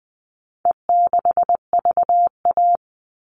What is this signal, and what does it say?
Morse code, heard as a keyed tone, about 19 wpm, 700 Hz. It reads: E6VA
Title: All Cut Numbers - 4-Digit 20wpm